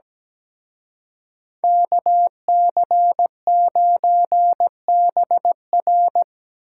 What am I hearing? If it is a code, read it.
Morse code, heard as a keyed tone, about 20 wpm, 700 Hz. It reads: KC9BR
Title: Speed Racing - USA 2x2 Call Signs 17wpm